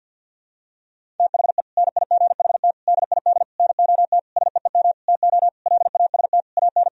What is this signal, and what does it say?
Morse code, heard as a keyed tone, about 51 wpm, 700 Hz. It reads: THE NIGHT DID NOT SEEM TO LAST AN
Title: Sentences from Top 1000 Words - Review Part 1 - 50wpm